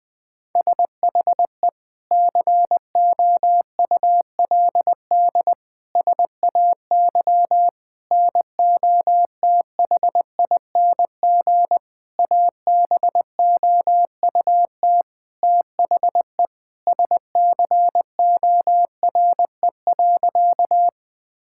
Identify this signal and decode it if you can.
Morse code, heard as a keyed tone, about 20 wpm, 700 Hz. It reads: SHE COULD SAY NOTHING ABOUT THE SCORE.
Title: Sentences from Top 1600 Words 20wpm